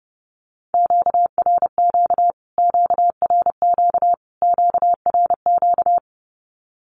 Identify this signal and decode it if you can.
Morse code, heard as a keyed tone, about 30 wpm, 700 Hz. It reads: QRQ QRQ QRQ
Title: Q-Codes Repeated 3x 30wpm